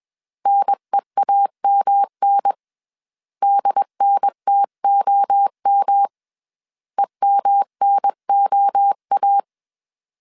Morse code, heard as a keyed tone, about 21 wpm, 800 Hz. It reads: NEAMN DNTOM EMNOA